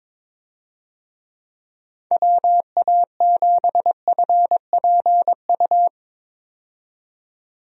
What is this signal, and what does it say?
Morse code, heard as a keyed tone, about 22 wpm, 700 Hz. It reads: WA7FPU
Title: USA Call Signs - 6 Characters - 22wpm